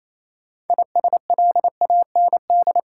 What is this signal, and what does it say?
Morse code, heard as a keyed tone, about 28 wpm, 700 Hz. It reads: ISLAND
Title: Sentences from Top 500 Words - New Word Drill - 28wpm